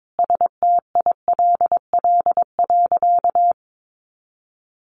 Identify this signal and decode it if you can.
Morse code, heard as a keyed tone, about 22 wpm, 700 Hz. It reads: STILL.
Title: Sentences from Top 1000 Words 22wpm